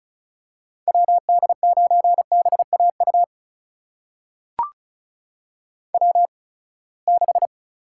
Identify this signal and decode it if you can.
Morse code, heard as a keyed tone, about 36 wpm, spaced as for 24 wpm, 700 Hz. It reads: WD9BAU W6
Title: USA Call Signs - 8x Character Spacing - 35wpm